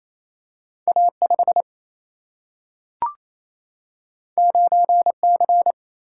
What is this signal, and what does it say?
Morse code, heard as a keyed tone, about 28 wpm, 700 Hz. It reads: A 5 9 C
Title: Single-Letter Single-Number 28wpm